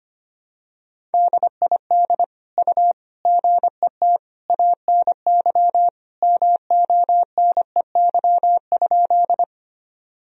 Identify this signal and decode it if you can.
Morse code, heard as a keyed tone, about 25 wpm, 700 Hz. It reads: DID U GET ANY MONEY?
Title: Sentences from Top 400 Words and Abbreviations 25wpm